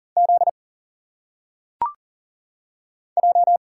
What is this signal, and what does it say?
Morse code, heard as a keyed tone, about 40 wpm, 700 Hz. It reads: Z J
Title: Single Letters - Phonetic Pronunciation 40wpm